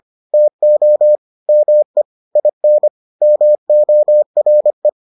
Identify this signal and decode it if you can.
Morse code, heard as a keyed tone, about 25 wpm, 600 Hz. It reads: TO ME IN MORE